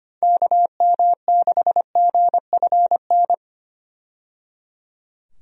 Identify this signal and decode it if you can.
Morse code, heard as a keyed tone, about 25 wpm, 700 Hz. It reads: KM6GFN
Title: USA Call Signs - 6 Characters - 25wpm